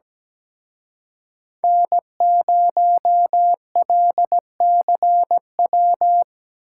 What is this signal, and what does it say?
Morse code, heard as a keyed tone, about 18 wpm, 700 Hz. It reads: N0LCW
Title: Speed Racing - USA Call Signs 17wpm